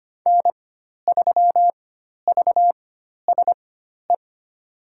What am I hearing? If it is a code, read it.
Morse code, heard as a keyed tone, about 25 wpm, spaced as for 12 wpm, 700 Hz. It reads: N3VSE
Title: USA Call Signs - 5 Characters - 4x Character Spacing - 25wpm